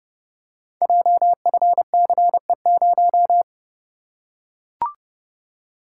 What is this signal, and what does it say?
Morse code, heard as a keyed tone, about 30 wpm, 700 Hz. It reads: JFCE0
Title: Code Groups - 5 Letter:Numbers - 30wpm